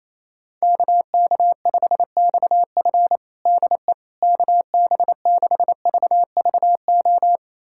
Morse code, heard as a keyed tone, about 28 wpm, 700 Hz. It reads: KK5XF DE KB6VVO
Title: USA Callsign DE Callsign 28wpm